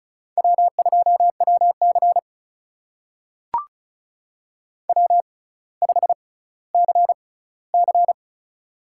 Morse code, heard as a keyed tone, about 34 wpm, spaced as for 18 wpm, 700 Hz. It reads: W2WC W5CC
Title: USA Call Signs - 4 Characters - 6x Character Spacing - 35wpm